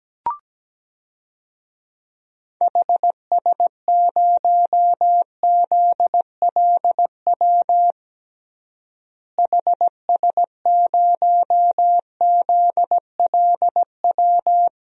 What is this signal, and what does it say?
Morse code, heard as a keyed tone, about 17 wpm, 700 Hz. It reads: HS0ZLW HS0ZLW
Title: DXCC-100 Call Signs - 6 Characters - Repeated 2x - 17wpm